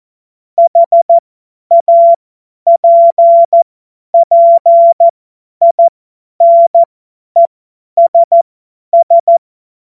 Morse code, heard as a keyed tone, about 14 wpm, spaced as for 10 wpm, 680 Hz. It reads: HAPPINESS
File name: Morsekurs / Witze / Witz 14. Dezember 2021, 50 F70